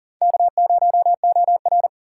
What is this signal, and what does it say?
Morse code, heard as a keyed tone, about 50 wpm, 700 Hz. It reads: K0OR